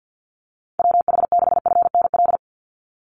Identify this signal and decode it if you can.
Morse code, heard as a keyed tone, about 50 wpm, 700 Hz. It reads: WH6FNL
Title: USA Call Signs 50wpm